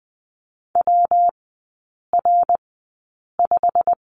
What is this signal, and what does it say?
Morse code, heard as a keyed tone, about 20 wpm, spaced as for 8 wpm, 700 Hz. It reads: WR5
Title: 114 Lesson 19e - Letter w - Three Characters 20wpm